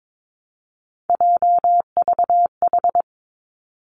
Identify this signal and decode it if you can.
Morse code, heard as a keyed tone, about 22 wpm, 700 Hz. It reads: JVH
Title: Three Challenging Letter-Number Combinations 22wpm